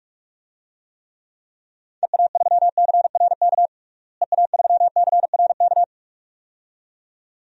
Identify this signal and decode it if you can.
Morse code, heard as a keyed tone, about 46 wpm, 700 Hz. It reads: EA3CRK EA3CRK
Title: DXCC-100 Call Signs - 6 Characters - Repeated 2x - 45wpm